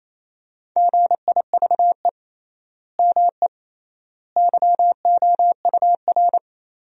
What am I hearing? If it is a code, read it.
Morse code, heard as a keyed tone, about 28 wpm, 700 Hz. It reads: GIVE ME YOUR
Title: 3x Word-Spacing - Sentences from Top 200 Words - 28wpm